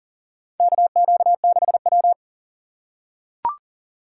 Morse code, heard as a keyed tone, about 40 wpm, 700 Hz. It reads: KQ6W
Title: USA Call Signs - 4 Characters - 6x Character Spacing - 40wpm